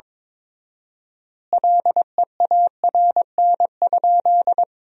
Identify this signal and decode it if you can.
Morse code, heard as a keyed tone, about 22 wpm, 700 Hz. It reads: LEARN?
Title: 4x Word-Spacing - Sentences from Top 300 Words - 22wpm